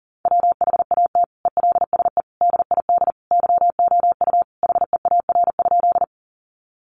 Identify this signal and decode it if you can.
Morse code, heard as a keyed tone, about 37 wpm, 690 Hz. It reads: WHAT ELSE DID YOU HEAR?
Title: Sentences from Top 800 Words 40wpm